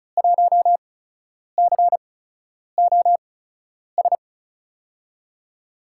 Morse code, heard as a keyed tone, about 34 wpm, spaced as for 10 wpm, 700 Hz. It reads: 1COS
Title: USA Call Signs - 5 Characters - 8x Character Spacing - 35wpm